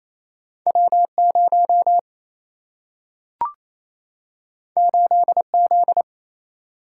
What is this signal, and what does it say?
Morse code, heard as a keyed tone, about 28 wpm, 700 Hz. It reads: W 0 8 Z
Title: Single-Letter Single-Number 28wpm